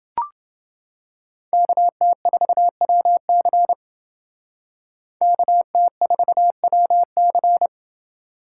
Morse code, heard as a keyed tone, about 28 wpm, 700 Hz. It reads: KT4WC KT4WC